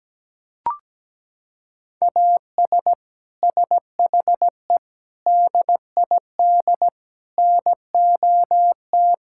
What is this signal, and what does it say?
Morse code, heard as a keyed tone, about 17 wpm, 700 Hz. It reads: AS SHE DID NOT
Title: Sentences from Top 1000 Words - Review Part 5 - 17wpm